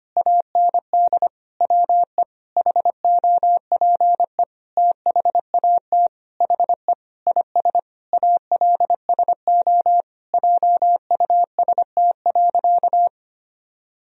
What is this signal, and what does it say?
Morse code, heard as a keyed tone, about 25 wpm, 700 Hz. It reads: AND WE HOPE THAT HE IS ALSO JUST.
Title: Sentences from Top 1000 Words - Review Part 1 - 25wpm